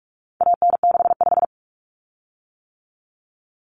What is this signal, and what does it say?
Morse code, heard as a keyed tone, about 44 wpm, 700 Hz. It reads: AN65
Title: Common Cut Numbers - 4-Digit 45wpm